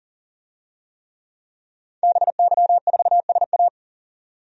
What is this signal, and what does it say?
Morse code, heard as a keyed tone, about 40 wpm, 700 Hz. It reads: BY4SA
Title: DXCC-100 Call Signs - 5 Characters - 40wpm